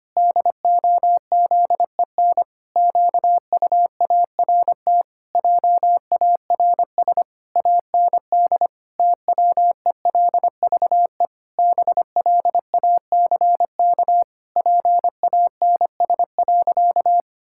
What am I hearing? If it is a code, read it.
Morse code, heard as a keyed tone, about 25 wpm, 700 Hz. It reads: DOZEN QUART JARS AND TWELVE BLACK PANS.
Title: Pangram Challenge v2 - 25wpm